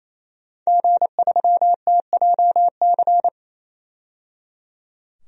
Morse code, heard as a keyed tone, about 27 wpm, 700 Hz. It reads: G3TJC
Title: DXCC-100 Call Signs - 5 Characters - 28wpm